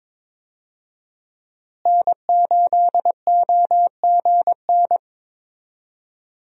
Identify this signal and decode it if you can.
Morse code, heard as a keyed tone, about 22 wpm, 700 Hz. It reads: N8OGN
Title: USA Call Signs - 5 Characters - Repeated 2x - 22wpm